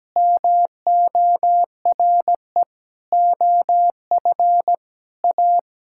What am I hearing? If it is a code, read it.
Morse code, heard as a keyed tone, about 17 wpm, 700 Hz. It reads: MORE OF A
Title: Sentences from Top 1000 Words - Review Part 5 - 17wpm